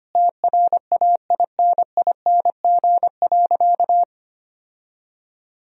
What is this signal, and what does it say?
Morse code, heard as a keyed tone, about 25 wpm, 700 Hz. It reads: TRAINING.
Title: Sentences from Top 1000 Words - Review Part 4 - 25wpm